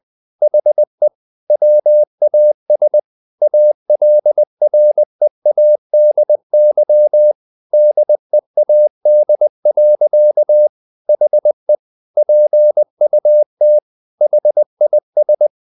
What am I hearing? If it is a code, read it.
Morse code, heard as a keyed tone, about 20 wpm, 600 Hz. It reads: HE WAS ALREADY DEAD. HE PUT HIS
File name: the old man and the sea part 2 20wpm